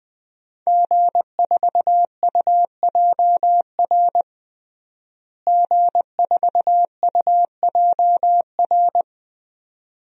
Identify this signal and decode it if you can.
Morse code, heard as a keyed tone, about 21 wpm, 700 Hz. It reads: G4UJR G4UJR
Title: DXCC-100 Call Signs - 5 Characters - Repeated 2x - 20wpm